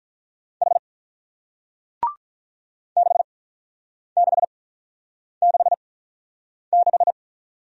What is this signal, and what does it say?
Morse code, heard as a keyed tone, about 48 wpm, 700 Hz. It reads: H 6 6 6 6